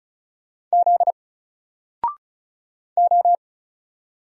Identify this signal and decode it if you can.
Morse code, heard as a keyed tone, about 35 wpm, 700 Hz. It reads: Z O
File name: Single-Letter-Number-35wpm